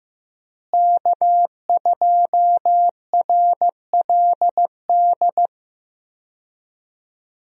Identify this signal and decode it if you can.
Morse code, heard as a keyed tone, about 15 wpm, 700 Hz. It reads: K2RLD
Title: USA Call Signs - 5 Characters - Repeated 2x - 15wpm